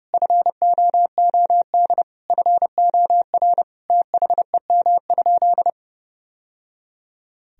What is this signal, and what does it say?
Morse code, heard as a keyed tone, about 30 wpm, 700 Hz. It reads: FOOD FOR THEM?
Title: Sentences from Top 1000 Words - Review Part 1 - 30wpm